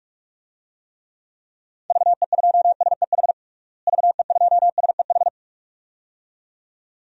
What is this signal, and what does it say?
Morse code, heard as a keyed tone, about 41 wpm, 690 Hz. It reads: VE2SEH VE2SEH
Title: DXCC-100 Call Signs - 6 Characters - Repeated 2x - 45wpm